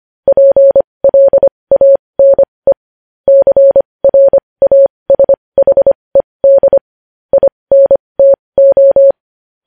Morse code, heard as a keyed tone, about 25 wpm, spaced as for 20 wpm, 550 Hz. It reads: PLANE CRASHED INTO
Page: news headlines in morse code at 20 wpm